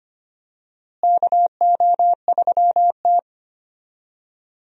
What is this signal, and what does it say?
Morse code, heard as a keyed tone, about 25 wpm, 700 Hz. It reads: KO3T
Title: USA Call Signs - 4 Characters - Repeated 2x - 25wpm